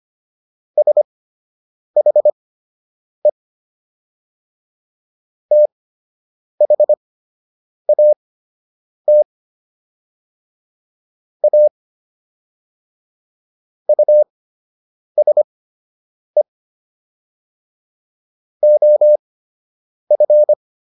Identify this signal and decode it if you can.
Morse code, heard as a keyed tone, about 25 wpm, spaced as for 8 wpm, 600 Hz. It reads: SHE THAT A USE OF